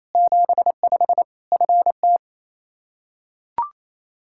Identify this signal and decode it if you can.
Morse code, heard as a keyed tone, about 28 wpm, 700 Hz. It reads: 75 FT
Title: Practice QSO - Exchange 2 - Info - 28wpm